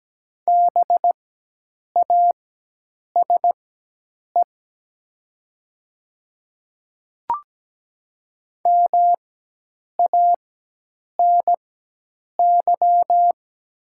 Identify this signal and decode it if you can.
Morse code, heard as a keyed tone, about 17 wpm, spaced as for 8 wpm, 700 Hz. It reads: BASE MANY
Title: 4x Character Spacing - Top 300 Words - 17wpm